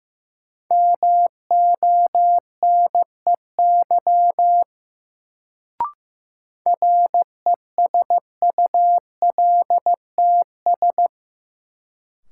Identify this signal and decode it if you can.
Morse code, heard as a keyed tone, about 15 wpm, 700 Hz. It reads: MONEY RESULTS
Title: Sets of 4 Words - New Word Drill - 15wpm